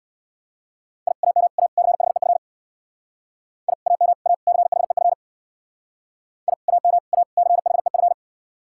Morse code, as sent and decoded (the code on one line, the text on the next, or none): .. / .- -- / .- / -.-. .... ..-. / .. / .- -- / .- / -.-. .... ..-. / .. / .- -- / .- / -.-. .... ..-.
I AM A CHEF I AM A CHEF I AM A CHEF